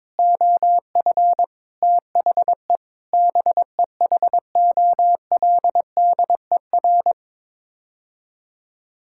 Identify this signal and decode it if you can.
Morse code, heard as a keyed tone, about 22 wpm, 700 Hz. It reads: OF THE BEHOLDER
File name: practice / 22wpm / Sets of 7 Words - 22wpm